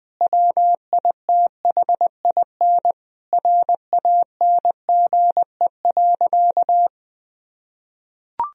Morse code, heard as a keyed tone, about 20 wpm, 700 Hz. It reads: WITHIN RANGE.
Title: Sentences from Top 1000 Words - Review Part 4 - 20wpm